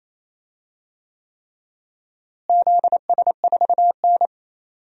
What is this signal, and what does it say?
Morse code, heard as a keyed tone, about 28 wpm, 700 Hz. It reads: ZS4N
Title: DXCC-100 Call Signs - 4 Characters - Repeated 3x - 28wpm